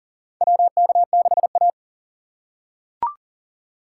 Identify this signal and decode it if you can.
Morse code, heard as a keyed tone, about 40 wpm, 700 Hz. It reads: WK6A
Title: USA Call Signs - 6x Character Spacing - 40wpm